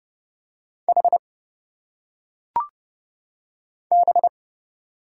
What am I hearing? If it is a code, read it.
Morse code, heard as a keyed tone, about 30 wpm, 700 Hz. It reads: H B
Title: Single Letters - Phonetic Pronunciation 30wpm